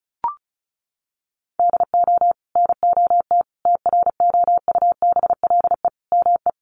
Morse code, heard as a keyed tone, about 34 wpm, 700 Hz. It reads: DO NOT TROUBLE ME
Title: Sentences from Top 1200 Words 35wpm